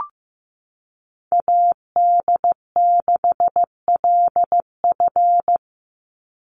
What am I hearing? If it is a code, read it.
Morse code, heard as a keyed tone, about 15 wpm, 700 Hz. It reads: AD6LF
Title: USA Call Signs 15wpm